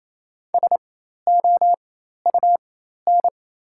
Morse code, heard as a keyed tone, about 27 wpm, spaced as for 13 wpm, 690 Hz. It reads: SOUN
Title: 4x Character Spacing - Top 300 Words - 28wpm